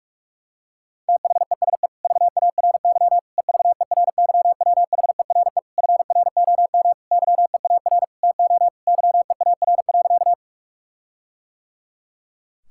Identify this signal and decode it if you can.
Morse code, heard as a keyed tone, about 46 wpm, 700 Hz. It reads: THESE VARY EVERYWHERE FROM YEAR TO YEAR.
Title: Sentences from Top 1600 Words 45wpm